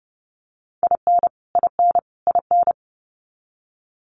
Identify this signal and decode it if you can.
Morse code, heard as a keyed tone, about 30 wpm, 700 Hz. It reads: IN IN IN
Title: Top 500 Words Repeated 30wpm